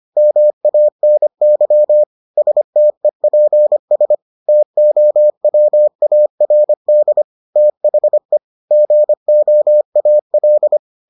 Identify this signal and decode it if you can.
Morse code, heard as a keyed tone, about 25 wpm, 600 Hz. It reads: MANY STEPS TOWARD THE GOAL